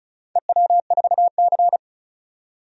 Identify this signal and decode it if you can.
Morse code, heard as a keyed tone, about 36 wpm, 700 Hz. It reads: EW4C
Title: DXCC-100 Call Signs - 4 Characters - 35wpm